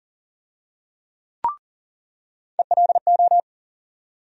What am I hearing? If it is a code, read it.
Morse code, heard as a keyed tone, about 40 wpm, 700 Hz. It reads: ELO
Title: USA Call Sign Postfixes - 3 Letters 40wpm